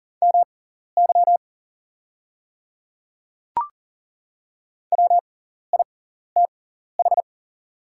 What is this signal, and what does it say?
Morse code, heard as a keyed tone, about 40 wpm, spaced as for 14 wpm, 690 Hz. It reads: MY WITH
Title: 6x Character Spacing - Top 100 Words - 40wpm